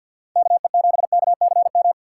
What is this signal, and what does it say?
Morse code, heard as a keyed tone, about 53 wpm, 700 Hz. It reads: KE7KCM
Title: USA Call Signs - 6 Characters - 6x Character Spacing - 50wpm